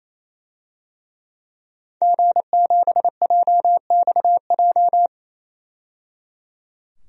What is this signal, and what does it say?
Morse code, heard as a keyed tone, about 28 wpm, 700 Hz. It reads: G7JXJ
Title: DXCC-100 Call Signs - 5 Characters - 28wpm